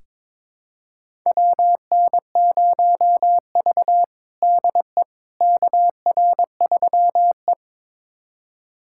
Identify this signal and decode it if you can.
Morse code, heard as a keyed tone, about 22 wpm, 700 Hz. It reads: WN0V DE KR3E
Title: USA Callsign DE Callsign - 4 Characters - 22wpm